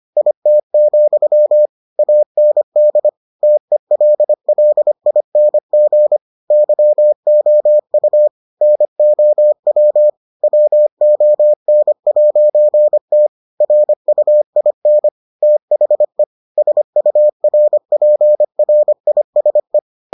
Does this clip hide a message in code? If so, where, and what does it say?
Yes: Morse code from 0.2 to 19.8 s, heard as a keyed tone, about 25 wpm, 600 Hz. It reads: IT, AND TELLING YOU NOW WON'T RUIN THE SURPRISE